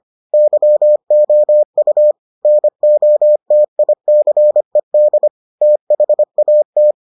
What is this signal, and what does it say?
Morse code, heard as a keyed tone, about 25 wpm, 600 Hz. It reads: YOU NOTICED THAT